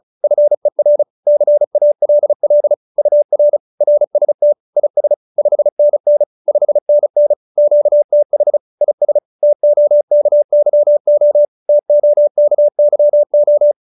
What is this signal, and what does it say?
Morse code, heard as a keyed tone, about 34 wpm, 590 Hz. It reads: FER CALL UR RST IS 5NN 5NN QTH IS TOKYO TOKYO